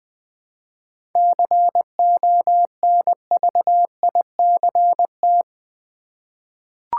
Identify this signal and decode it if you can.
Morse code, heard as a keyed tone, about 20 wpm, 700 Hz. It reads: CONVICT
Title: 7 Letter Words - First 3 Letters Given - 20wpm